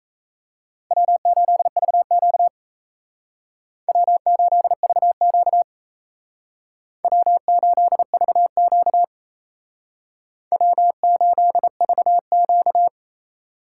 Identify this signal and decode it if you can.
Morse code, heard as a keyed tone, about 36 wpm, 700 Hz. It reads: W8VQ W8VQ W8VQ W8VQ